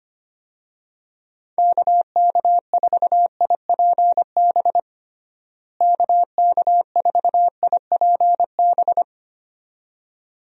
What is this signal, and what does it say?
Morse code, heard as a keyed tone, about 25 wpm, 700 Hz. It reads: KK4IPB KK4IPB
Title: USA Call Signs - 6 Characters - Repeated 2x - 25wpm